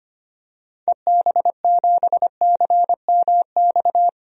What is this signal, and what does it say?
Morse code, heard as a keyed tone, about 25 wpm, 700 Hz. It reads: EB7CMX
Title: DXCC-100 Call Signs - 6 Characters - 25wpm